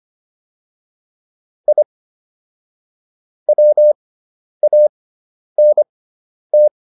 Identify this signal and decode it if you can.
Morse code, heard as a keyed tone, about 25 wpm, spaced as for 10 wpm, 600 Hz. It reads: I WANT